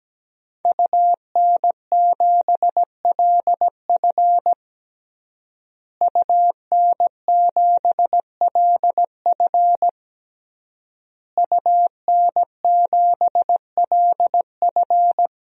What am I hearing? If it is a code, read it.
Morse code, heard as a keyed tone, about 17 wpm, 700 Hz. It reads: UN7LF UN7LF UN7LF
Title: DXCC-100 Call Signs - 5 Characters - Repeated 3x - 17wpm